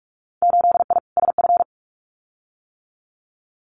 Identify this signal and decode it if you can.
Morse code, heard as a keyed tone, about 46 wpm, 700 Hz. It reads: 8I SF
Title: Winter Field Day Exchanges Encore 45wpm